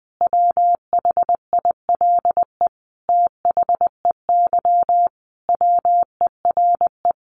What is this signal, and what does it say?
Morse code, heard as a keyed tone, about 19 wpm, 700 Hz. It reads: WHILE THEY WERE
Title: Sentences from Top 1000 Words 20wpm